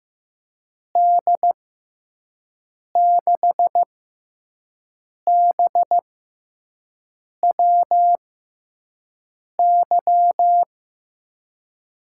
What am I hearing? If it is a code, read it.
Morse code, heard as a keyed tone, about 15 wpm, 700 Hz. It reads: D 6 B W Y
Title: USA Call Signs - 6x Character Spacing - 15wpm